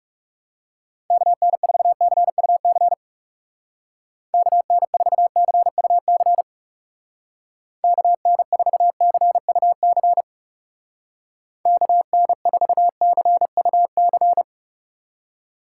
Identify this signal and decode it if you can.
Morse code, heard as a keyed tone, about 37 wpm, 700 Hz. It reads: KN4CUC KN4CUC KN4CUC KN4CUC
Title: Speed Racing - USA 2x3 Call Signs 30wpm